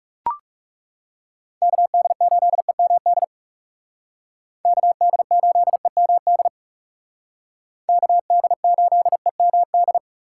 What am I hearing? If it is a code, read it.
Morse code, heard as a keyed tone, about 34 wpm, 700 Hz. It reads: KD8EMD KD8EMD KD8EMD
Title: Speed Racing - USA Call Signs 30wpm